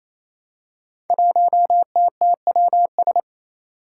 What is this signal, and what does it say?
Morse code, heard as a keyed tone, about 27 wpm, 700 Hz. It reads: 1TTWS
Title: Five Challenging Letter-Number Combinations 28wpm